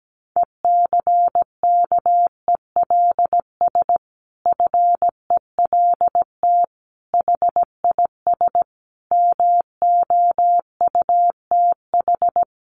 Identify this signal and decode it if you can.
Morse code, heard as a keyed tone, about 17 wpm, 700 Hz. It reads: ECKELS FELT HIS MOUTH
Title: A Sound Of Thunder 17wpm